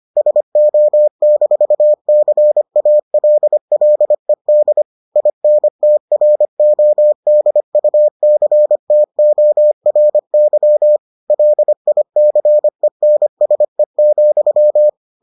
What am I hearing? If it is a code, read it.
Morse code, heard as a keyed tone, about 25 wpm, 600 Hz. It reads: SO-CALLED INTRODUCTORY LICENSE,